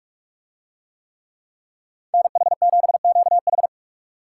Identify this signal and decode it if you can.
Morse code, heard as a keyed tone, about 44 wpm, 700 Hz. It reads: NH7QH